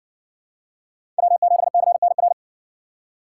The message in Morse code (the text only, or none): W7CNR